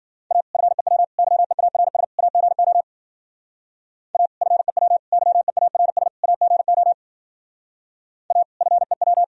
Morse code, heard as a keyed tone, about 50 wpm, 700 Hz. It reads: A FEW YEARS AGO A FEW YEARS AGO A FEW
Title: Speed-Racing - Sets of 4 Words 40wpm